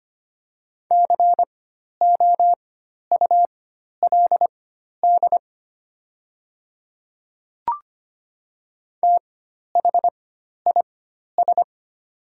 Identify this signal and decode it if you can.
Morse code, heard as a keyed tone, about 25 wpm, spaced as for 12 wpm, 690 Hz. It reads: COULD THIS
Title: 4x Character Spacing - Top 100 Words - 25wpm